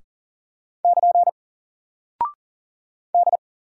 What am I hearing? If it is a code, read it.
Morse code, heard as a keyed tone, about 40 wpm, 700 Hz. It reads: <KN> D
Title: Single Character (Letter:Number:Punctuation:Prosign) 40wpm